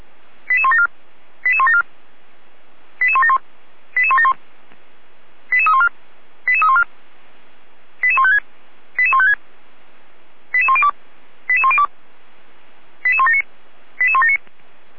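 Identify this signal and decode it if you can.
Five-tone selective call (ZVEI), heard as quick runs of five tones, tones of about 70 ms. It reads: 88174 88174 88171 88171 88314 88314 88156 88156 88102 88102 88179 88179
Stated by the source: Received by SDR; ZVEI five-tone selective call.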